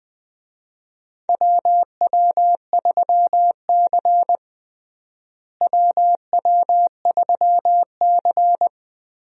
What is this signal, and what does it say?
Morse code, heard as a keyed tone, about 21 wpm, 700 Hz. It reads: WW3C WW3C
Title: USA Call Signs - 4 Characters - Repeated 3x - 20wpm